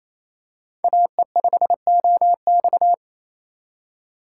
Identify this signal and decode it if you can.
Morse code, heard as a keyed tone, about 28 wpm, 690 Hz. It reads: AE5OX
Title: USA Call Signs - 5 Characters - 4x Character Spacing - 28wpm